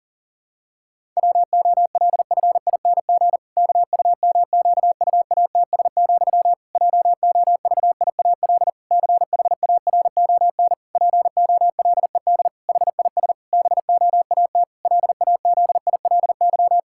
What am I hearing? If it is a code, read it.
Morse code, heard as a keyed tone, about 37 wpm, 700 Hz. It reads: WOLFING KUMQUATS, JOVIAL CHARON POLED HIS BOAT LAZILY
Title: Pangram Challenge v2 - 40wpm